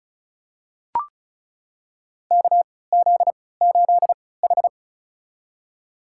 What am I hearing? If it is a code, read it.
Morse code, heard as a keyed tone, about 36 wpm, spaced as for 20 wpm, 700 Hz. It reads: KZ8H
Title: USA Call Signs - 4 Characters - 3x Character Spacing - 35wpm